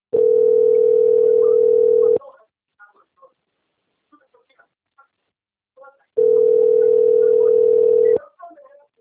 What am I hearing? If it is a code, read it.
Ringback tone (North America), 440 + 480 Hz, with speech.
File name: out- - - - - .6630